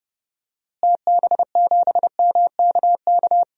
Morse code, heard as a keyed tone, about 32 wpm, 700 Hz. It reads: TB7MKK
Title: DXCC-100 Call Signs - 6 Characters - Repeated 2x - 30wpm